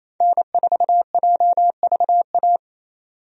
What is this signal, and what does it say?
Morse code, heard as a keyed tone, about 28 wpm, 700 Hz. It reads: N4JVA
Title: USA Call Signs - 5 Characters - 28wpm